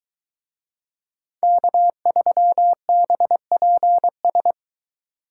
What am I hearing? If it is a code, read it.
Morse code, heard as a keyed tone, about 22 wpm, 700 Hz. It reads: K3BPS